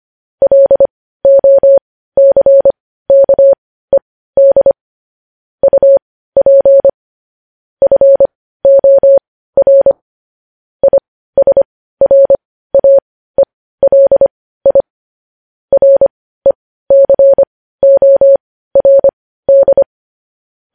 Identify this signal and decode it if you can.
Morse code, heard as a keyed tone, about 25 wpm, spaced as for 15 wpm, 550 Hz. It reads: LOCKED UP FOR ISRAELI RECORD
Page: news-headlines-in-morse-code-at-15-wpm